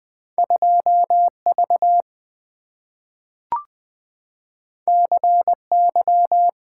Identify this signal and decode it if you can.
Morse code, heard as a keyed tone, about 20 wpm, 700 Hz. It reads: 2 V C Y
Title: Two Letter-Number Combinations - 20wpm